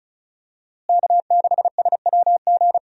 Morse code, heard as a keyed tone, about 36 wpm, 700 Hz. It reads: K6SWG
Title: USA Call Signs - 5 Characters - Repeated 3x - 35wpm